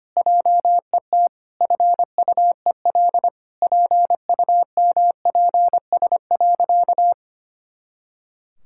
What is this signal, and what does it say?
Morse code, heard as a keyed tone, about 25 wpm, 700 Hz. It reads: JET FUEL PUMPS.
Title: Pangram Challenge v2 - 25wpm